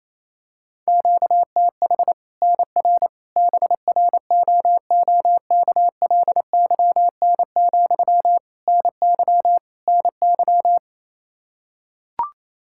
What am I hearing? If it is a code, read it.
Morse code, heard as a keyed tone, about 28 wpm, 700 Hz. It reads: QTH NR BROOKLYN, NY NY
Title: Practice QSO - Exchange 1 - Beginning - 28wpm